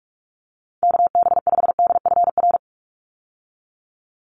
Morse code, heard as a keyed tone, about 44 wpm, 700 Hz. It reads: KB5DFR
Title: USA Call Signs 45wpm